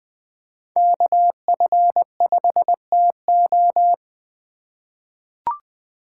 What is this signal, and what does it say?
Morse code, heard as a keyed tone, about 20 wpm, 700 Hz. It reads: KF5TO
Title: USA Call Signs - 8x Character Spacing - 20wpm